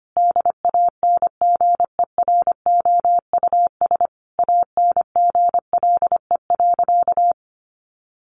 Morse code, heard as a keyed tone, about 25 wpm, 700 Hz. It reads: DANGEROUS ANGLE.
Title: Sentences from Top 1500 Words 25wpm